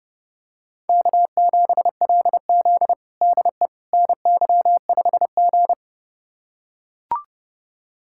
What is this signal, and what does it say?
Morse code, heard as a keyed tone, about 30 wpm, 700 Hz. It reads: K7LZ DE NY5G
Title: USA Callsign DE Callsign - 4 Characters - 30wpm